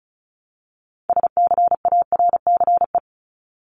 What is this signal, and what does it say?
Morse code, heard as a keyed tone, about 34 wpm, 700 Hz. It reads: SCARCE
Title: LY Suffix - Top 100 Words with Base Words 35wpm